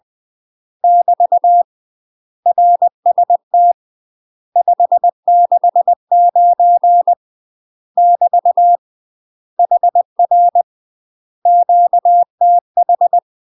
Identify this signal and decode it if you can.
Morse code, heard as a keyed tone, about 20 wpm, 700 Hz. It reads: <BT> RST 569 <BT> HR QTH
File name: UK MASTER QSOs TRAINING/20WPM